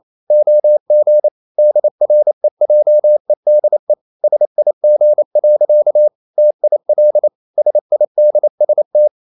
Morse code, heard as a keyed tone, about 28 wpm, 600 Hz. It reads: OG DREJEDE SIG. TIL SIDST
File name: Ferie QRQ 28wpm